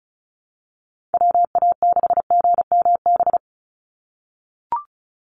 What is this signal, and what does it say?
Morse code, heard as a keyed tone, about 34 wpm, 700 Hz. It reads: WA6GMB
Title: USA Call Signs 35wpm